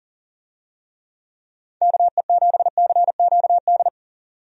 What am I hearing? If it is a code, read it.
Morse code, heard as a keyed tone, about 37 wpm, 700 Hz. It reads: KE7CQD
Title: USA Call Signs - 6 Characters - 40wpm